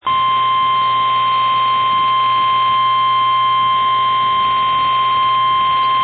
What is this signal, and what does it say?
Emergency Alert System (SAME) tones (the 1050 Hz attention tone), with noise. It is heard from 0.3 to 5.5 s.
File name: eas / KNCYKBIE / KNCYKBIE